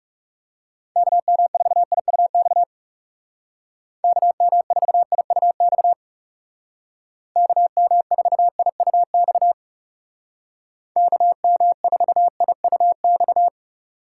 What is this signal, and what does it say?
Morse code, heard as a keyed tone, about 38 wpm, 700 Hz. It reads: KM4IUX KM4IUX KM4IUX KM4IUX
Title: Speed Racing - USA 2x3 Call Signs 30wpm